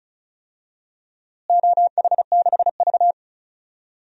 Morse code, heard as a keyed tone, about 36 wpm, 700 Hz. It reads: OH6V
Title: DXCC-100 Call Signs - 4 Characters - Repeated 2x - 35wpm